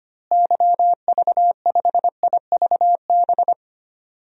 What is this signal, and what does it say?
Morse code, heard as a keyed tone, about 25 wpm, 700 Hz. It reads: YV5IVB
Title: DXCC-100 Call Signs - 6 Characters - Repeated 3x - 25wpm